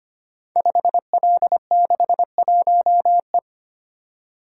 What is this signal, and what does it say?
Morse code, heard as a keyed tone, about 25 wpm, 700 Hz. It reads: 5L61E
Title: Code Groups - 5 Letter:Numbers - 25wpm